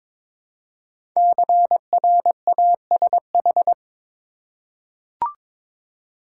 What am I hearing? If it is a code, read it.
Morse code, heard as a keyed tone, about 22 wpm, 700 Hz. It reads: CRASH
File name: 5 Letter Words - First 3 Letters Given - 22wpm